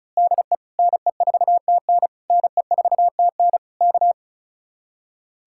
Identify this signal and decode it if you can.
Morse code, heard as a keyed tone, about 32 wpm, 700 Hz. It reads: DE NE4TN NE4TN K
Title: Speed-Racing Common QSO Phrases - 35wpm